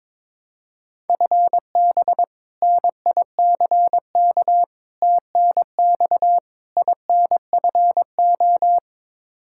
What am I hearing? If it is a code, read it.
Morse code, heard as a keyed tone, about 22 wpm, 700 Hz. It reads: FB NICK TNX INFO
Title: Practice QSO - Exchange 3 - More Info - 22wpm